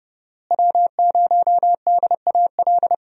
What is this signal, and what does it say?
Morse code, heard as a keyed tone, about 30 wpm, 700 Hz. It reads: W0DAL
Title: USA Call Signs - 5 Characters - Repeated 3x - 30wpm